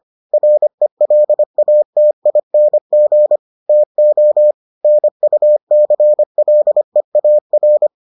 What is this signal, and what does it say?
Morse code, heard as a keyed tone, about 25 wpm, 600 Hz. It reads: RELATING TO NUCLEAR